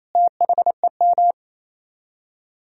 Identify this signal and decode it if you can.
Morse code, heard as a keyed tone, about 35 wpm, 700 Hz. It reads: THEM
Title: Speed-Racing - Sets of 4 Words 28wpm